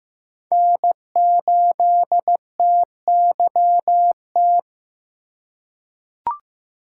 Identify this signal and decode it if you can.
Morse code, heard as a keyed tone, about 15 wpm, 700 Hz. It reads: N8TYT
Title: USA Call Signs - 5 Characters - 2x Character Spacing - 15wpm